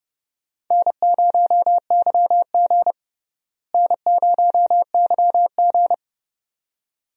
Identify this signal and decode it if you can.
Morse code, heard as a keyed tone, about 31 wpm, 700 Hz. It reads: N0YG N0YG
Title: USA Call Signs - 4 Characters - Repeated 2x - 30wpm